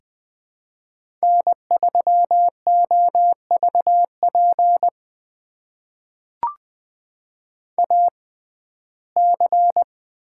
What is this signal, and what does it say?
Morse code, heard as a keyed tone, about 20 wpm, spaced as for 14 wpm, 700 Hz. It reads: N3OVP AC
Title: USA Call Signs - 5 Characters - 6x Character Spacing - 20wpm